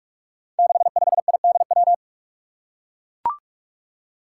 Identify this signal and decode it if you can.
Morse code, heard as a keyed tone, about 44 wpm, 700 Hz. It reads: 65IDW
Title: Five Challenging Letter-Number Combinations 45wpm